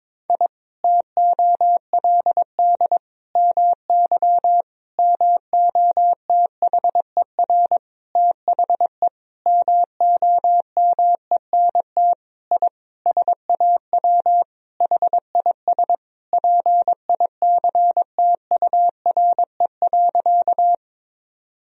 Morse code, heard as a keyed tone, about 22 wpm, 700 Hz. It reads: I TOLD MY MOTHER THE MOMENT I SAW HIS PICTURE.
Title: Sentences from Top 1000 Words - Review Part 1 - 22wpm